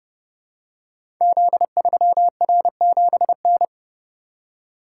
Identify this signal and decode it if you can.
Morse code, heard as a keyed tone, about 30 wpm, 700 Hz. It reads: Z3R7N
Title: Code Groups - 5 Letter:Numbers - 30wpm